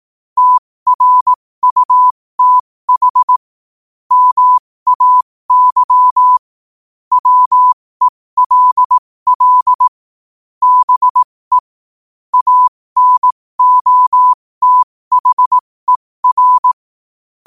Morse code, heard as a keyed tone, about 18 wpm, spaced as for 15 wpm, 1000 Hz. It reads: TRUTH MAY WELL BE ANOTHER
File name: qotdfiles / 15wpm